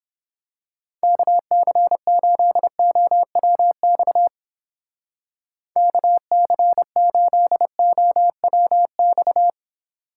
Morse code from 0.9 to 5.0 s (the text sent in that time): KC8OWX